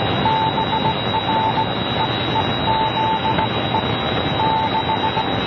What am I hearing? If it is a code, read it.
Morse code, heard as a keyed tone, about 17 wpm, spaced as for 15 wpm, 880 Hz, with radio noise. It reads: BREEZE 6